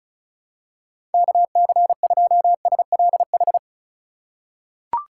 Morse code, heard as a keyed tone, about 36 wpm, 700 Hz. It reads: KC2SLH
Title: USA Call Signs - 2x Character Spacing - 35wpm